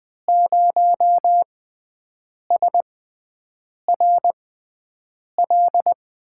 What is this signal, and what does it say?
Morse code, heard as a keyed tone, about 21 wpm, spaced as for 7 wpm, 700 Hz. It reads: 0SRL
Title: USA Call Signs - 5 Characters - 6x Character Spacing - 20wpm